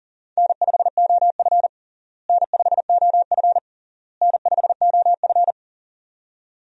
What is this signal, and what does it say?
Morse code, heard as a keyed tone, about 37 wpm, 700 Hz. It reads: N5OF N5OF N5OF
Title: USA Call Signs - 4 Characters - Repeated 3x - 40wpm